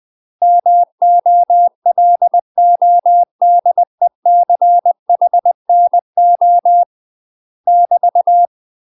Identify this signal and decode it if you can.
Morse code, heard as a keyed tone, about 20 wpm, 700 Hz. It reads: MOLODECHNO <BT>
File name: UK MASTER QSOs TRAINING / 20WPM / UK MASTER QSOs 0070